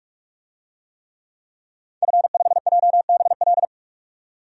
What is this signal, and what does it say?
Morse code, heard as a keyed tone, about 44 wpm, 700 Hz. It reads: F5JBL
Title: DXCC-100 Call Signs - 5 Characters - 45wpm